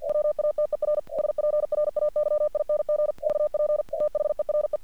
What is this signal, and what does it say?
Morse code, heard as a keyed tone, about 48 wpm, 600 Hz. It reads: WATER SPRAYING VP THERE
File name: archived / CRSnet